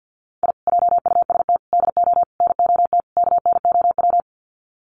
Morse code, heard as a keyed tone, about 50 wpm, 700 Hz. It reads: I JUST DO NOT KNOW
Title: Sets of 5 Words Encore 50wpm